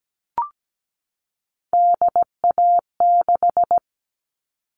Morse code, heard as a keyed tone, about 17 wpm, 700 Hz. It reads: DA6
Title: All Cut Numbers - 3-Digit 17wpm